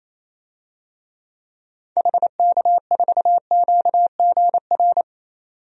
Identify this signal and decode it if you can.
Morse code, heard as a keyed tone, about 28 wpm, 700 Hz. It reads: HK4QGR
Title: DXCC-100 Call Signs - 6 Characters - 28wpm